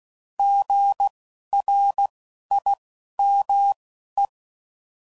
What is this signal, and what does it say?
Morse code, heard as a keyed tone, about 16 wpm, spaced as for 11 wpm, 800 Hz. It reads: GRIME
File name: Prefixes / By First Letter